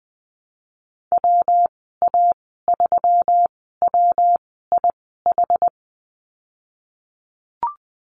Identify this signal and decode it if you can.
Morse code, heard as a keyed tone, about 20 wpm, spaced as for 15 wpm, 700 Hz. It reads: WA3WIH
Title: 122 Lesson 20g - Number 3 - Call Signs 20wpm